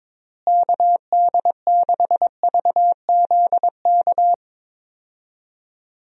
Morse code, heard as a keyed tone, about 22 wpm, 700 Hz. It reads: KD6VZK
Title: USA Call Signs - 6 Characters - Repeated 3x - 22wpm